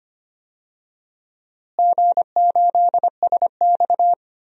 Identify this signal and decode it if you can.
Morse code, heard as a keyed tone, about 25 wpm, 700 Hz. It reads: G8SX
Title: DXCC-100 Call Signs - 4 Characters - Repeated 2x - 25wpm